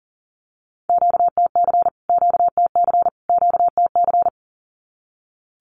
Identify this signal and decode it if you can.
Morse code, heard as a keyed tone, about 40 wpm, 700 Hz. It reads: QTC QTC QTC
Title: Q-Codes Repeated 3x 40wpm